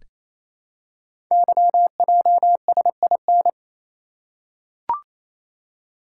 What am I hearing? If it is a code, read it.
Morse code, heard as a keyed tone, about 27 wpm, 700 Hz. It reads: YJSIN
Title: Code Groups - 5 Letters - 28wpm